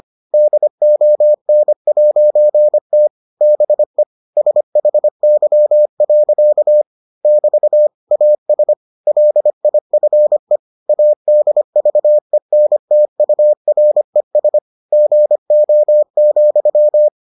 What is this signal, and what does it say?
Morse code, heard as a keyed tone, about 25 wpm, 600 Hz. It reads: DON'T BE SHY. <BT> AS LIFE ADVENTURES GO,